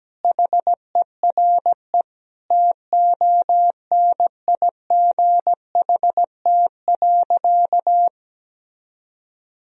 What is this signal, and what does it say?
Morse code, heard as a keyed tone, about 17 wpm, 700 Hz. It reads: HERE TONIGHT.
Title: Sentences from Top 1000 Words - Review Part 3 - 17wpm